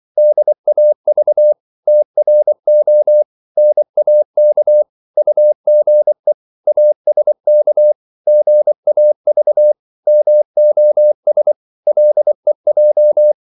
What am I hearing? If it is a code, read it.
Morse code, heard as a keyed tone, about 24 wpm, 600 Hz. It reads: DAV TRO NAK UGE ASK GAV MOS LEJ
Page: Korte danske ord 24wpm